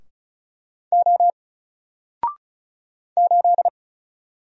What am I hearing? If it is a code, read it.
Morse code, heard as a keyed tone, about 36 wpm, 700 Hz. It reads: O 8
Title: Single Letter-Number Alt 35wpm